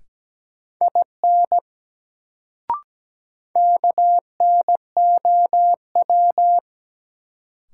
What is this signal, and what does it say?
Morse code, heard as a keyed tone, about 17 wpm, 700 Hz. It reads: IN KNOW
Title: Sentences from Top 100 Words - New Word Drill - 17wpm